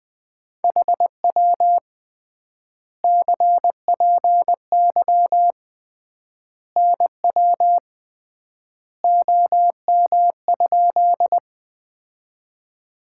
Morse code, heard as a keyed tone, about 20 wpm, 700 Hz. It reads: HW CPY NW OM?
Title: Common QSO Phrases - 3x Word-Spacing - 20wpm